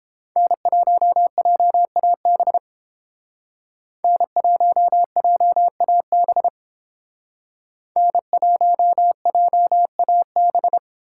Morse code, heard as a keyed tone, about 27 wpm, 700 Hz. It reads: N1JAB N1JAB N1JAB
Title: Speed Racing - USA Call Signs 22wpm